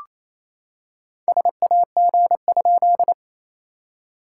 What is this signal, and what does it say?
Morse code, heard as a keyed tone, about 27 wpm, 700 Hz. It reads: SAG?
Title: SOTA QSO Practice 28wpm